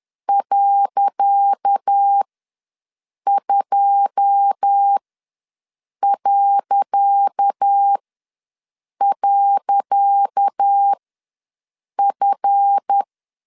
Morse code, heard as a keyed tone, about 10 wpm, spaced as for 6 wpm, 800 Hz. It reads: .2..F
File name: LEZIONI / 10 SIGNS / 10C DOT ws 10wpm 4213